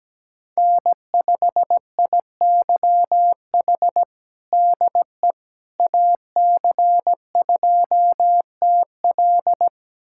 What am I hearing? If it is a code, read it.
Morse code, heard as a keyed tone, about 17 wpm, 700 Hz. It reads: N5IYH DE AC2TL
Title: USA Callsign DE Callsign - 5 Characters - 17wpm